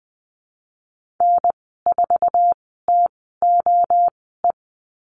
Morse code, heard as a keyed tone, about 20 wpm, spaced as for 14 wpm, 700 Hz. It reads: N4TOE
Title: 069 Lesson 12g - Letter d - Call Signs 20wpm